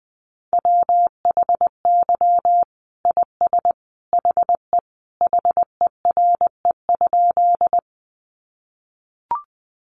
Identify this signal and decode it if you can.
Morse code, heard as a keyed tone, about 20 wpm, 700 Hz. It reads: WHY IS HE HERE?
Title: Sentences from Top 200 Words 20wpm